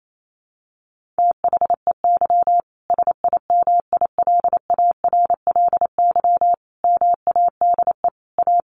Morse code, heard as a keyed tone, about 28 wpm, 700 Hz. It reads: THEY SIMILARLY MADE A
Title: Sentences from Top 1400 Words 28wpm